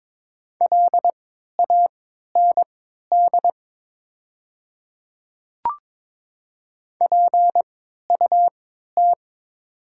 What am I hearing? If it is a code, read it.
Morse code, heard as a keyed tone, about 22 wpm, spaced as for 12 wpm, 700 Hz. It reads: LAND PUT
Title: 3x Character Spacing - Top 300 Words - 22wpm